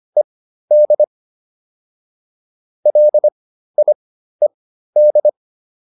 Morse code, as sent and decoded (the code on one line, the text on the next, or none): . -.. / .-.. .. -..
ED LIED